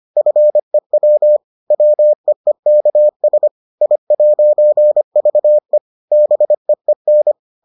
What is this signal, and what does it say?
Morse code, heard as a keyed tone, about 25 wpm, 600 Hz. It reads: FEW WEEKS I'VE BEEN